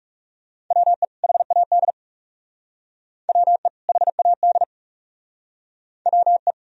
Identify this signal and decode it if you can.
Morse code, heard as a keyed tone, about 37 wpm, 700 Hz. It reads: WE HAD WE HAD WE
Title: Speed-Racing - Sets of 2 Words 30wpm